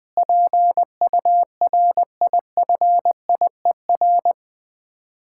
Morse code, heard as a keyed tone, about 21 wpm, 700 Hz. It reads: PURIFIER
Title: Easy Words - Speed-Racing with Hints - 20wpm